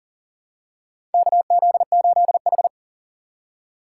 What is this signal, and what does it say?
Morse code, heard as a keyed tone, about 40 wpm, 700 Hz. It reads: KZ8H
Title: USA Call Signs - 4 Characters - 6x Character Spacing - 40wpm